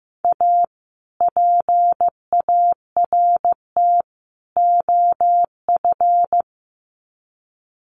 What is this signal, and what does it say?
Morse code, heard as a keyed tone, about 15 wpm, 700 Hz. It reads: A PART OF
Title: Sets of 3 Words 15wpm